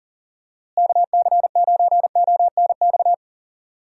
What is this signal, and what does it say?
Morse code, heard as a keyed tone, about 39 wpm, 700 Hz. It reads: KC9ONX
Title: USA Call Signs - 6 Characters - Repeated 3x - 40wpm